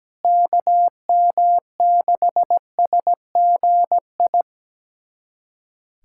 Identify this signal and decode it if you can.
Morse code, heard as a keyed tone, about 17 wpm, 700 Hz. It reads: KM6SGI
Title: USA Callsign DE Callsign 17wpm